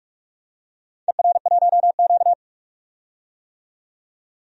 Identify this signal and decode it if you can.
Morse code, heard as a keyed tone, about 44 wpm, 700 Hz. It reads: ER1Q
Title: DXCC-100 Call Signs - 4 Characters - 45wpm